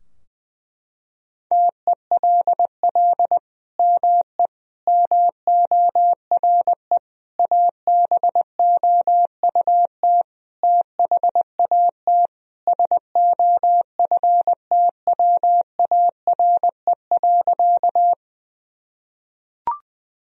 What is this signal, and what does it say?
Morse code, heard as a keyed tone, about 21 wpm, 700 Hz. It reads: TELL ME MORE ABOUT THAT SOFTWARE.
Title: Sentences from Top 1000 Words - Review Part 3 - 20wpm